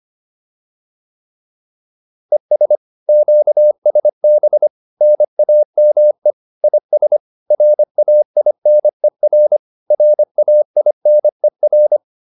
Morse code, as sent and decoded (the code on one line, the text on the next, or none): . ... / --.- ... -... / -. .- -- . / .. ... / .-. .- .. -. .-. / .-. .- .. -. .-.
ES QSB NAME IS RAINER RAINER